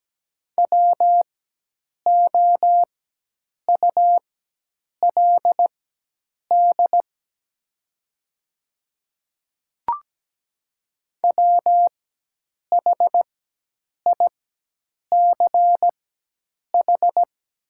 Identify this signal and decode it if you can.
Morse code, heard as a keyed tone, about 17 wpm, spaced as for 8 wpm, 700 Hz. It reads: WOULD WHICH
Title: 4x Character Spacing - Top 200 Words - 17wpm